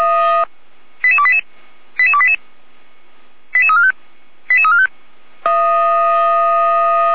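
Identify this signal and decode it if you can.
Five-tone selective call (ZVEI), heard as quick runs of five tones, tones of about 70 ms. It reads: 88288 88288 88345 88345
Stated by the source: ZVEI five-tone selective call 88211 received by SDR.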